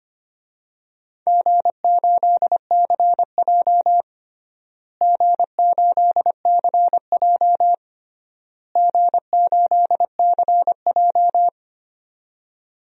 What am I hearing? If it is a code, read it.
Morse code, heard as a keyed tone, about 25 wpm, 700 Hz. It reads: G8CJ G8CJ G8CJ
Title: DXCC-100 Call Signs - 4 Characters - Repeated 3x - 25wpm